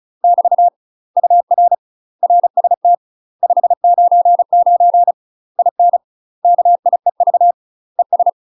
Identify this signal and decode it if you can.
Morse code, heard as a keyed tone, about 35 wpm, 700 Hz. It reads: <BT> UR RST 599 IN KIEV ES
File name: UK MASTER QSOs TRAINING / 35WPM / UK MASTER QSOs 0022